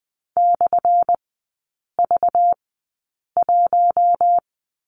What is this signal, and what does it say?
Morse code, heard as a keyed tone, about 20 wpm, 700 Hz. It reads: / V 1
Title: 198 Lesson 32e - Letter k - Three Characters 20wpm